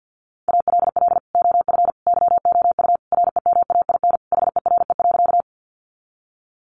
Morse code, heard as a keyed tone, about 50 wpm, 700 Hz. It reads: ALL OF YOU REMAIN HERE.
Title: Sentences from Top 1200 Words 50wpm